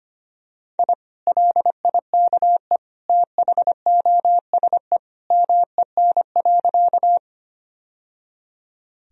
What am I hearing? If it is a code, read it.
Morse code, heard as a keyed tone, about 25 wpm, 700 Hz. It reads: I LIKE THOSE MEN.
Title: Sentences from Top 1000 Words - Review Part 1 - 25wpm